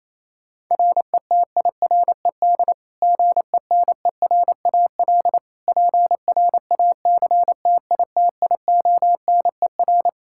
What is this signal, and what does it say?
Morse code, heard as a keyed tone, about 27 wpm, 700 Hz. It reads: RETIRED GENERAL PRACTITIONER
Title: Practice QSO - Exchange 3 - More Info - 28wpm